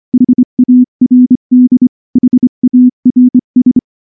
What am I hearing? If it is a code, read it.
Morse code, heard as a keyed tone, about 25 wpm, 250 Hz. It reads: HARD HARS
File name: morse / morse-training-25WPM